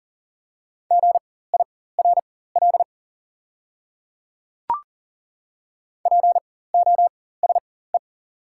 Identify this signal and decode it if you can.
Morse code, heard as a keyed tone, about 40 wpm, spaced as for 19 wpm, 690 Hz. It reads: GIRL POSE
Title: 4x Character Spacing - Top 500 Words - 40wpm